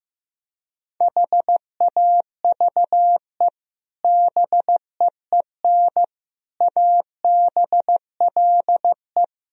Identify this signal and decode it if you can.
Morse code, heard as a keyed tone, about 16 wpm, 700 Hz. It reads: HAVE BEEN ABLE
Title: Speed-Racing - Sets of 4 Words 15wpm